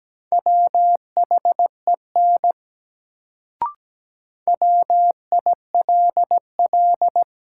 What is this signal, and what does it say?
Morse code, heard as a keyed tone, about 17 wpm, 700 Hz. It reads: WHEN WILL
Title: Sentences from Top 100 Words - New Word Drill - 17wpm